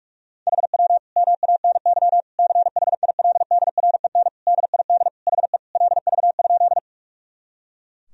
Morse code, heard as a keyed tone, about 46 wpm, 700 Hz. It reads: HW MANY CHILDREN DID HE LV?
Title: Sentences from Top 400 Words and Abbreviations 45wpm